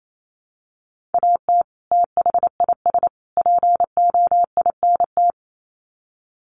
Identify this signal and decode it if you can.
Morse code, heard as a keyed tone, about 28 wpm, 700 Hz. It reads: AT THIS POINT
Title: Sets of 3 Words 28wpm